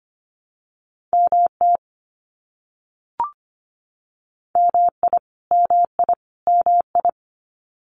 Morse code, heard as a keyed tone, about 25 wpm, 700 Hz. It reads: MT MI MI MI
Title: US States Repeated 3x 25wpm